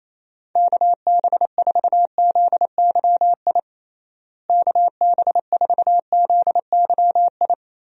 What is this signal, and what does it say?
Morse code, heard as a keyed tone, about 28 wpm, 700 Hz. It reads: KB4ZYI KB4ZYI
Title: USA Call Signs - 6 Characters - Repeated 2x - 28wpm